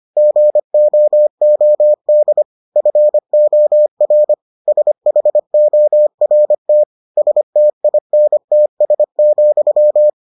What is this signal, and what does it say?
Morse code, heard as a keyed tone, about 25 wpm, 600 Hz. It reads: GOOD FOR SHORT STINTS,